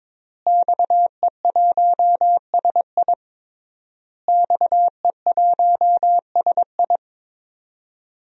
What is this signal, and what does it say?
Morse code, heard as a keyed tone, about 22 wpm, 700 Hz. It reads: XE1SI XE1SI
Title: DXCC-100 Call Signs - 5 Characters - Repeated 3x - 22wpm